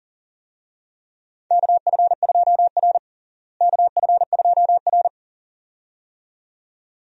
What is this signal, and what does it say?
Morse code, heard as a keyed tone, about 40 wpm, 700 Hz. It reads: KF2R KF2R
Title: USA Call Signs - 4 Characters - Repeated 2x - 40wpm